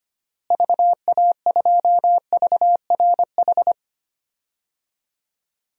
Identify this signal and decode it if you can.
Morse code, heard as a keyed tone, about 25 wpm, 700 Hz. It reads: VA2VRH
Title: DXCC-100 Call Signs - 6 Characters - Repeated 3x - 25wpm